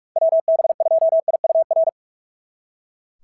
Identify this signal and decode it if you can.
Morse code, heard as a keyed tone, about 44 wpm, 690 Hz. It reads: WB2IUR
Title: SOTA QSO Practice 45wpm